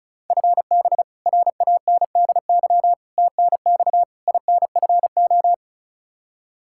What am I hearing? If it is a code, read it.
Morse code, heard as a keyed tone, about 36 wpm, 700 Hz. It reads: FB RANDY TNX INFO
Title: K1USN SST QSO Practice 35wpm